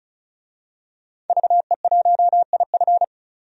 Morse code, heard as a keyed tone, about 34 wpm, 700 Hz. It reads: VE1IF
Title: DXCC-100 Call Signs - 5 Characters - 35wpm